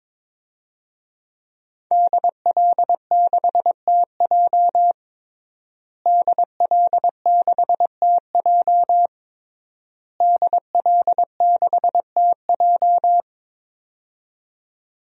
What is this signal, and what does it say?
Morse code, heard as a keyed tone, about 22 wpm, 700 Hz. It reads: DL6TJ DL6TJ DL6TJ
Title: DXCC-100 Call Signs - 5 Characters - Repeated 3x - 22wpm